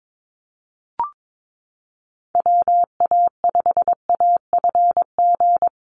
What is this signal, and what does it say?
Morse code, heard as a keyed tone, about 22 wpm, 700 Hz. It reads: WA5AFG
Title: USA Call Signs Repeated 2x 22wpm